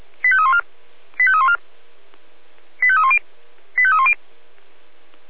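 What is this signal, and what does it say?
Five-tone selective call (ZVEI), heard as quick runs of five tones, tones of about 70 ms. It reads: 85314 85314 85310 85310
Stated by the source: Received on SDR as ZVEI five-tone selective call.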